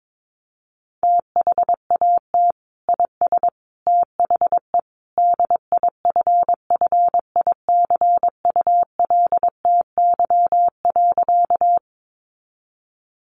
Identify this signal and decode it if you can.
Morse code, heard as a keyed tone, about 22 wpm, 700 Hz. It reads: THAT IS THE DIFFICULTY.
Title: Sentences from Top 1500 Words 22wpm